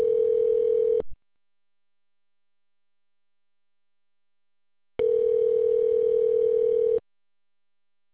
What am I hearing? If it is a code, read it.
Ringback tone (North America), 440 + 480 Hz.